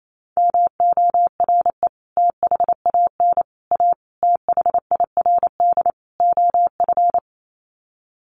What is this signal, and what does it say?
Morse code, heard as a keyed tone, about 28 wpm, 700 Hz. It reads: MORE THAN A THIRD OF
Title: Sets of 5 Words Encore2 28wpm